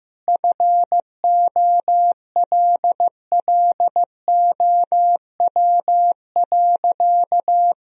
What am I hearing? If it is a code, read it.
Morse code, heard as a keyed tone, about 15 wpm, 700 Hz. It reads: FOLLOW.
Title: Sentences from Top 1000 Words - Review Part 3 - 15wpm